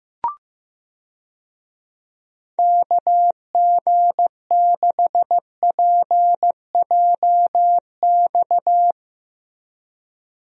Morse code, heard as a keyed tone, about 15 wpm, 700 Hz. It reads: KG6PJX
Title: USA Call Signs - 6 Characters - Repeated 2x - 15wpm